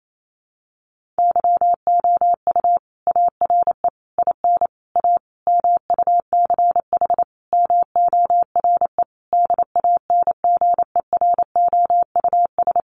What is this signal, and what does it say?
Morse code, heard as a keyed tone, about 28 wpm, 700 Hz. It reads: YOU ARE IN A MUCH MORE DANGEROUS
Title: Sentences from Top 1400 Words 28wpm